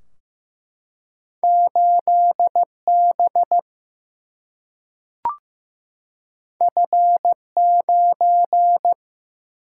Morse code, heard as a keyed tone, about 16 wpm, 700 Hz. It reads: 8 B F 9
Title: Single-Letter Single-Number 15wpm